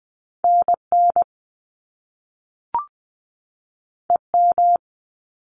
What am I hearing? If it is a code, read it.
Morse code, heard as a keyed tone, about 20 wpm, 700 Hz. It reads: N N E M
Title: Top Two Letter Combinations in Words 20wpm